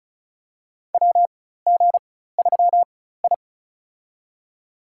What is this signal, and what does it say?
Morse code, heard as a keyed tone, about 36 wpm, spaced as for 16 wpm, 700 Hz. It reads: WG3I
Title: USA Call Signs - 4x Character Spacing - 35wpm